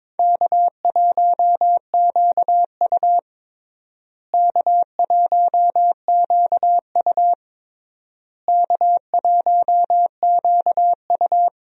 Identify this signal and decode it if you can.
Morse code, heard as a keyed tone, about 22 wpm, 700 Hz. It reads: K1QU K1QU K1QU
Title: USA Call Signs - 4 Characters - Repeated 3x - 22wpm